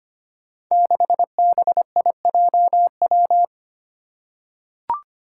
Morse code, heard as a keyed tone, about 25 wpm, 700 Hz. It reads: 6BIJW
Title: Five Challenging Letter-Number Combinations 25wpm